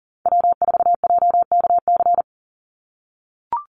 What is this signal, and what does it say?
Morse code, heard as a keyed tone, about 38 wpm, 700 Hz. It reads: W4JKC
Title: NA SOTA Chaser Callsigns 40wpm